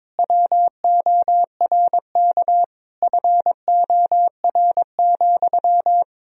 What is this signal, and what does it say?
Morse code, heard as a keyed tone, about 22 wpm, 700 Hz. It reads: WORK FOR,
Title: Sentences from Top 1000 Words - Review Part 4 - 22wpm